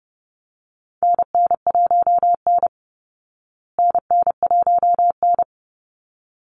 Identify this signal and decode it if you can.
Morse code, heard as a keyed tone, about 30 wpm, 700 Hz. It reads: NN1N NN1N
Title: Call Signs Repeated 3x - 30wpm